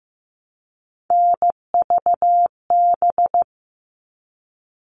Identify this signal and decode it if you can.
Morse code, heard as a keyed tone, about 15 wpm, 700 Hz. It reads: NVB
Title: All Cut Numbers - 3-Digit 15wpm